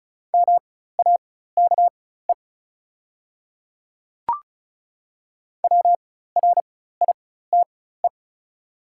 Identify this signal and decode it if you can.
Morse code, heard as a keyed tone, about 34 wpm, spaced as for 16 wpm, 690 Hz. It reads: MAKE WRITE
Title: 4x Character Spacing - Top 100 Words - 35wpm